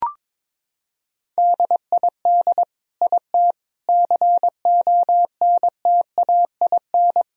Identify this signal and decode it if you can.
Morse code, heard as a keyed tone, about 22 wpm, 700 Hz. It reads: DID IT CONTAIN
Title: Sentences from Top 1000 Words - Review Part 2 - 22wpm